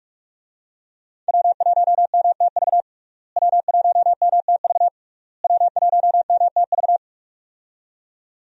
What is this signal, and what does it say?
Morse code, heard as a keyed tone, about 44 wpm, 700 Hz. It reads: W1MTV W1MTV W1MTV
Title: USA Call Signs - 5 Characters - Repeated 3x - 45wpm